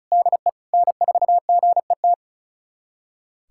Morse code, heard as a keyed tone, about 34 wpm, 700 Hz. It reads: DE N4GET
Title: USA Callsign DE Callsign - 5 Characters - 35wpm